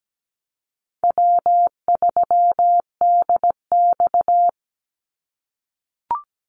Morse code, heard as a keyed tone, about 17 wpm, 700 Hz. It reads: W3DX
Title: NA SOTA Activator Callsigns 17wpm